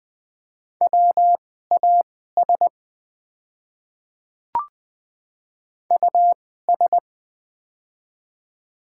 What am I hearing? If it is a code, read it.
Morse code, heard as a keyed tone, about 20 wpm, spaced as for 14 wpm, 700 Hz. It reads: WAS US
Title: 2x Character Spacing - Top 200 Words - 20wpm